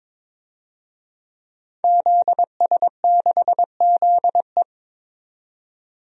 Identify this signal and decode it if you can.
Morse code, heard as a keyed tone, about 22 wpm, 700 Hz. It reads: ZS6ZE
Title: DXCC-100 Call Signs - 5 Characters - 22wpm